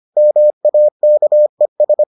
Morse code, heard as a keyed tone, about 25 wpm, 600 Hz. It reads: MAKES